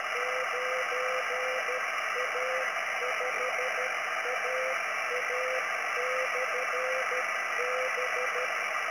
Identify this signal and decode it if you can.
Morse code, heard as a keyed tone, about 13 wpm, 510 Hz, with radio noise. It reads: 9A5AA/B